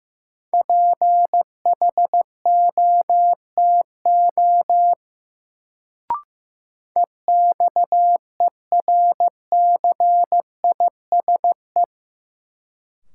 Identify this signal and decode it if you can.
Morse code, heard as a keyed tone, about 15 wpm, 700 Hz. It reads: PHOTO EXERCISE
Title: Sentences from Top 1200 Words - New Word Drill Alt - 15wpm